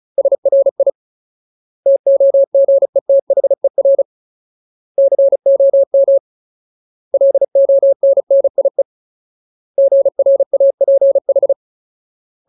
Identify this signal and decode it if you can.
Morse code, heard as a keyed tone, about 34 wpm, 550 Hz. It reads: SRI TOGETHER COM LONNIE GRAPH
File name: Ultimate Word List / 35WPM Ultimate